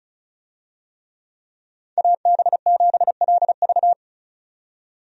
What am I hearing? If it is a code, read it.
Morse code, heard as a keyed tone, about 36 wpm, 700 Hz. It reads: AB7LV
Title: USA Call Signs - 5 Characters - 35wpm